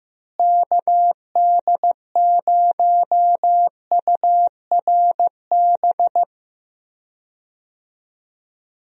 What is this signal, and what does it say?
Morse code, heard as a keyed tone, about 15 wpm, 700 Hz. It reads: KD0URB
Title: USA Call Signs - 6 Characters - Repeated 3x - 15wpm